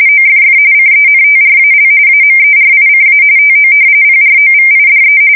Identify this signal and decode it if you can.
RTTY, about 45.45 baud. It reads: OFF CENTRAL AMERICA TO PROVIDE
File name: fsk170 / battleshipnj